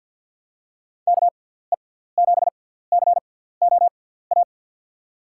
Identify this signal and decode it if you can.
Morse code, heard as a keyed tone, about 53 wpm, spaced as for 17 wpm, 700 Hz. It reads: KE7COA
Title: USA Call Signs - 6 Characters - 6x Character Spacing - 50wpm